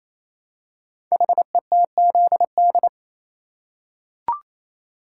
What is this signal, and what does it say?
Morse code, heard as a keyed tone, about 27 wpm, 700 Hz. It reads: HETZD
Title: Code Groups - 5 Letters - 28wpm